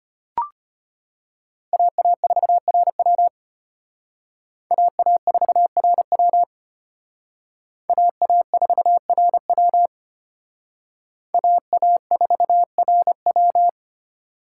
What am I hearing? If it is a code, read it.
Morse code, heard as a keyed tone, about 32 wpm, 700 Hz. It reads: AA4RW AA4RW AA4RW AA4RW